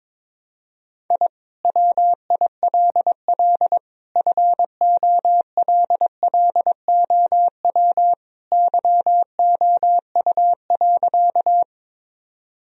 Morse code, heard as a keyed tone, about 22 wpm, 700 Hz. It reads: I WILL FOLLOW YOU.
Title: Sentences from Top 1000 Words - Review Part 1 - 22wpm